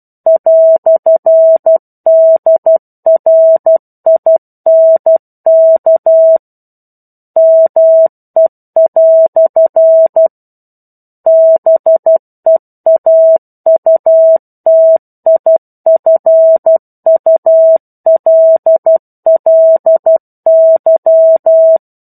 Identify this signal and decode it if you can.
Morse code, heard as a keyed tone, about 12 wpm, 650 Hz. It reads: "DRINK ME" BEAUTIFULLY
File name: 01-Chapter-01 12 WPM